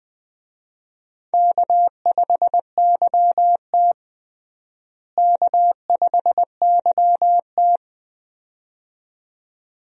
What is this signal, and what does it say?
Morse code, heard as a keyed tone, about 20 wpm, 700 Hz. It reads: K5YT K5YT
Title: USA Call Signs - 4 Characters - Repeated 2x - 20wpm